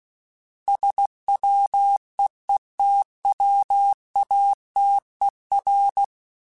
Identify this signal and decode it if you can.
Morse code, heard as a keyed tone, about 16 wpm, 800 Hz. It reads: SWEETWATER
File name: fromSMs / FromSMs mp3s / Cities@15